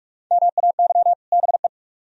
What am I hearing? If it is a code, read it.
Morse code, heard as a keyed tone, about 44 wpm, 700 Hz. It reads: MAY BE